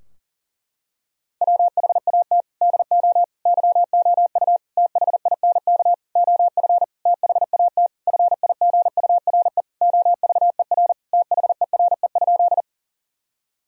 Morse code, heard as a keyed tone, about 40 wpm, 700 Hz. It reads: WHAT DO YOU THINK OF THAT FIGURE OVER THERE?
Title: Sentences from Top 1000 Words - Review Part 2 - 40wpm